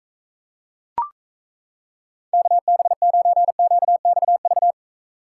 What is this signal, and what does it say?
Morse code, heard as a keyed tone, about 36 wpm, 700 Hz. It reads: KB9QXV